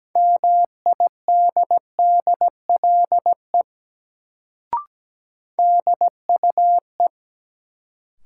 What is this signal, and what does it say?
Morse code, heard as a keyed tone, about 17 wpm, 700 Hz. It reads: MIDDLE DUE
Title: Sentences from Top 900 Words - New Word Drill - 17wpm